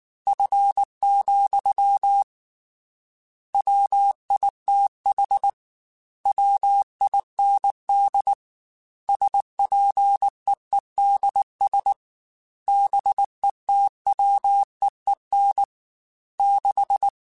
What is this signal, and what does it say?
Morse code, heard as a keyed tone, about 19 wpm, 800 Hz. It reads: F, WITH WIND SPEEDS BETWEEN 6
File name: Weather Sets / Weather Sets-mp3s / Weather 4 24 @18